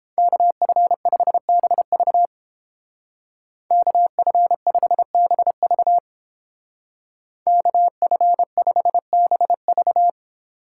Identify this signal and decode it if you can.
Morse code, heard as a keyed tone, about 28 wpm, 700 Hz. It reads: KF5BV KF5BV KF5BV